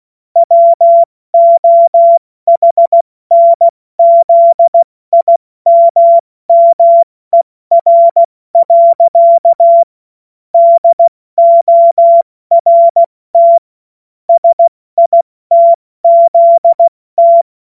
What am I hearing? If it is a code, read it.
Morse code, heard as a keyed tone, about 16 wpm, spaced as for 14 wpm, 680 Hz. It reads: WOHNZIMMER. DORT SITZT
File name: Morsekurs / Witze / Witz 24. August 2021, 70 F80